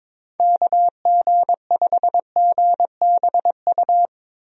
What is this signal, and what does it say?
Morse code, heard as a keyed tone, about 22 wpm, 700 Hz. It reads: KG5GBU
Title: USA Call Signs - 6 Characters - 22wpm